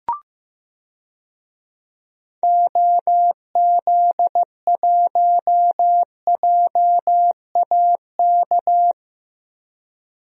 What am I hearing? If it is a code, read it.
Morse code, heard as a keyed tone, about 15 wpm, 700 Hz. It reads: OZ1JAK
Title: DXCC-100 Call Signs - 6 Characters - Repeated 2x - 15wpm